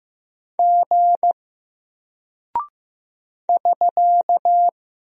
Morse code, heard as a keyed tone, about 15 wpm, 700 Hz. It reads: G <SK>
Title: Single Character (Letter:Number:Punctuation:Prosign) Alt 15wpm